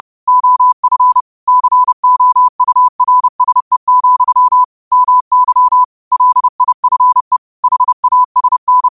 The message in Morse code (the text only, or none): OF COURSE, MY LIFE HASN